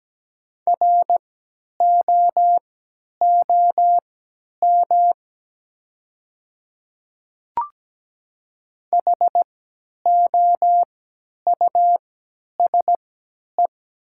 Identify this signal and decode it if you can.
Morse code, heard as a keyed tone, about 17 wpm, spaced as for 10 wpm, 690 Hz. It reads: ROOM HOUSE
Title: 3x Character Spacing - Top 300 Words - 17wpm